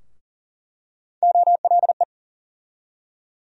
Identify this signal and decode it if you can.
Morse code, heard as a keyed tone, about 40 wpm, 700 Hz. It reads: OLE
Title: USA Call Sign Postfixes - 3 Letters 40wpm